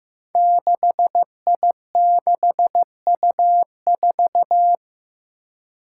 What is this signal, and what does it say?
Morse code, heard as a keyed tone, about 16 wpm, 700 Hz. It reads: 6I6U4
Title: Five Challenging Letter-Number Combinations 15wpm